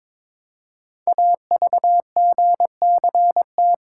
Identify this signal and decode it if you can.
Morse code, heard as a keyed tone, about 22 wpm, 700 Hz. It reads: AVGCT
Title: Code Groups - 5 Letter:Numbers - 22wpm